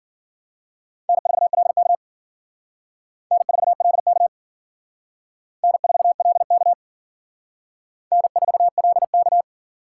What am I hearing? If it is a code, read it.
Morse code, heard as a keyed tone, about 50 wpm, 700 Hz. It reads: N4LK N4LK N4LK N4LK
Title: Speed Racing - USA 1x2 Call Signs 40wpm